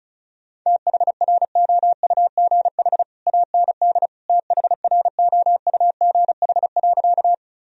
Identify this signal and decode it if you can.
Morse code, heard as a keyed tone, about 34 wpm, 700 Hz. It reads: THROUGH AND THROUGH.
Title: Sentences from Top 1000 Words - Review Part 1 - 35wpm